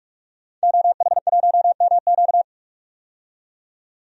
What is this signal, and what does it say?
Morse code, heard as a keyed tone, about 46 wpm, 700 Hz. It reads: OH1MQ
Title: DXCC-100 Call Signs - 5 Characters - 45wpm